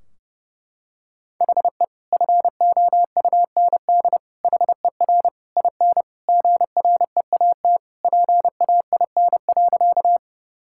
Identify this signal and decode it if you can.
Morse code, heard as a keyed tone, about 30 wpm, 700 Hz. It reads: HE FOUND HER IN GREAT PAIN.
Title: Sentences from Top 1600 Words 30wpm